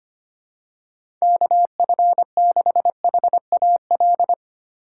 Morse code, heard as a keyed tone, about 25 wpm, 700 Hz. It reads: KF6HAL
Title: USA Call Signs - 6 Characters - 3x Character Spacing - 25wpm